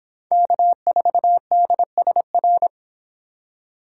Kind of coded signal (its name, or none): Morse code